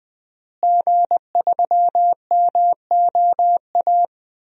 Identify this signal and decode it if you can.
Morse code, heard as a keyed tone, about 21 wpm, 700 Hz. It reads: G3MOA
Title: DXCC-100 Call Signs - 5 Characters - Repeated 3x - 20wpm